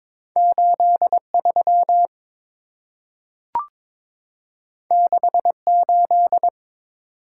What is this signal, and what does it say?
Morse code, heard as a keyed tone, about 23 wpm, 690 Hz. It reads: 83 68
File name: practice / 22wpm / Two Letter-Number Combinations - 22wpm